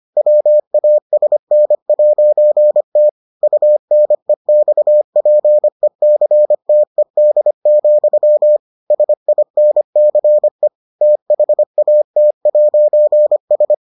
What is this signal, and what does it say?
Morse code, heard as a keyed tone, about 25 wpm, 600 Hz. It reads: WASN'T UNEXPECTED, SINCE THAT'S